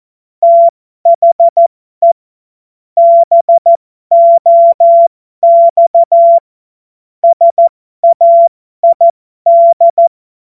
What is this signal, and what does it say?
Morse code, heard as a keyed tone, about 14 wpm, spaced as for 12 wpm, 680 Hz. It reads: THE BOX SAID
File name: Morsekurs / Witze